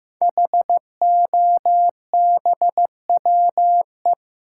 Morse code, heard as a keyed tone, about 15 wpm, 700 Hz. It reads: HOBWE
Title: Code Groups - 5 Letters - 15wpm